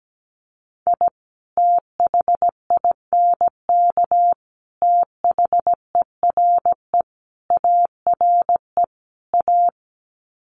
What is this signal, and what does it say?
Morse code, heard as a keyed tone, about 17 wpm, 700 Hz. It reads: I THINK THERE ARE A
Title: Sets of 5 Words Encore2 17wpm